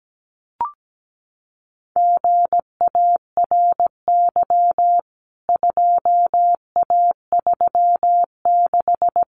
Morse code, heard as a keyed tone, about 17 wpm, 700 Hz. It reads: GARY 2A36
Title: CWT Exchange 17wpm